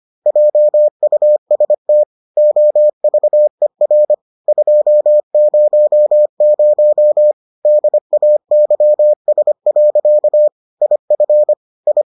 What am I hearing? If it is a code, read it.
Morse code, heard as a keyed tone, about 25 wpm, 600 Hz. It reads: JUST OVER 200 DAYS. IF I